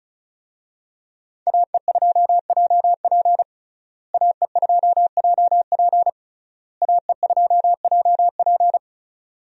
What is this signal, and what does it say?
Morse code, heard as a keyed tone, about 36 wpm, 700 Hz. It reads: AE2JP AE2JP AE2JP
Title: USA Call Signs - 5 Characters - Repeated 3x - 35wpm